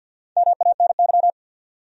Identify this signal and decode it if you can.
Morse code, heard as a keyed tone, about 53 wpm, 700 Hz. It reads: MANY
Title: Top 100 Words 50wpm